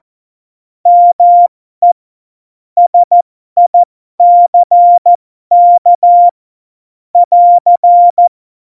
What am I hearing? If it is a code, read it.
Morse code, heard as a keyed tone, about 14 wpm, spaced as for 12 wpm, 700 Hz. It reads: ME SICK <AR>
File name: Morsekurs / Witze